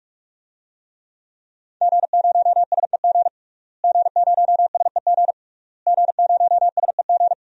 Morse code, heard as a keyed tone, about 46 wpm, 700 Hz. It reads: G0SEG G0SEG G0SEG
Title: DXCC-100 Call Signs - 5 Characters - Repeated 3x - 45wpm